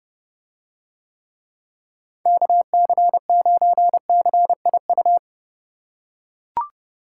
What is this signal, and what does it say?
Morse code, heard as a keyed tone, about 30 wpm, 700 Hz. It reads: KC9CIU